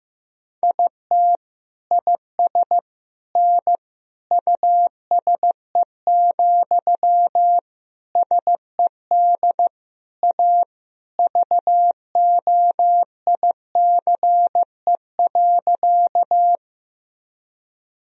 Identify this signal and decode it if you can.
Morse code, heard as a keyed tone, about 15 wpm, 700 Hz. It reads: IT IS N USE, SED A VOICE.
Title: Sentences from Top 500 Words and Abbreviations 15wpm